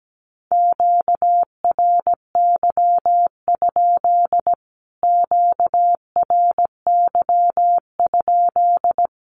Morse code, heard as a keyed tone, about 17 wpm, 700 Hz. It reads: QRY? QRY?
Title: Q-Codes Repeated 3x 17wpm